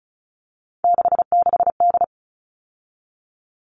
Morse code, heard as a keyed tone, about 34 wpm, 700 Hz. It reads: 66D
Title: All Cut Numbers - 3-Digit 35wpm